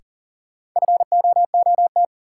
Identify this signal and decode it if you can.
Morse code, heard as a keyed tone, about 38 wpm, 700 Hz. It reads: FOOT
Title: Sentences from Top 500 Words - New Word Drill - 40wpm